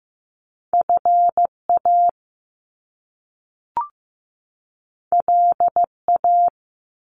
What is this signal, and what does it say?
Morse code, heard as a keyed tone, about 15 wpm, 700 Hz. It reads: FA LA
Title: Top Two Letter Combinations in Words 15wpm